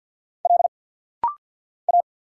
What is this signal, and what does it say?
Morse code, heard as a keyed tone, about 50 wpm, 700 Hz. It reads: L A
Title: Single Letters - Rapid-Fire 50wpm